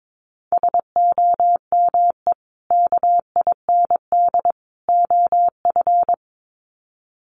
Morse code, heard as a keyed tone, about 22 wpm, 700 Hz. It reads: SOME KIND OF
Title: Sets of 3 Words 22wpm